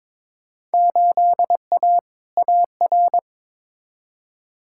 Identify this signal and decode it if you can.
Morse code, heard as a keyed tone, about 22 wpm, 700 Hz. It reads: 8A AR
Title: Field Day Exchanges - 22wpm